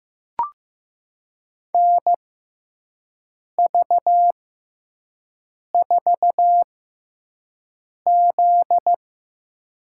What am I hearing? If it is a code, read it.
Morse code, heard as a keyed tone, about 15 wpm, spaced as for 5 wpm, 700 Hz. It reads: NV4Z
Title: USA Call Signs - 4 Characters - 6x Character Spacing - 15wpm